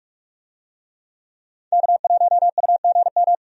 Morse code, heard as a keyed tone, about 44 wpm, 700 Hz. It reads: K1UGM
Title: USA Call Signs - 5 Characters - 45wpm